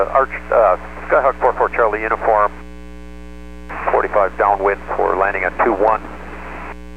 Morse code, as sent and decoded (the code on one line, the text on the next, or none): none